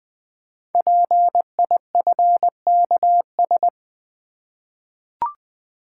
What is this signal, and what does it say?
Morse code, heard as a keyed tone, about 20 wpm, 700 Hz. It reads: PIFKS
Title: Code Groups - 5 Letters - 20wpm